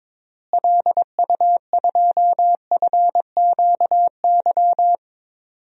Morse code, heard as a keyed tone, about 22 wpm, 700 Hz. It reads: LU2FQY
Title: DXCC-100 Call Signs - 6 Characters - Repeated 2x - 22wpm